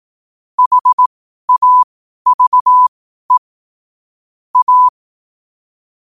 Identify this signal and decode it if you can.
Morse code, heard as a keyed tone, about 18 wpm, spaced as for 12 wpm, 1000 Hz. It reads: HAVE A
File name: 12wpm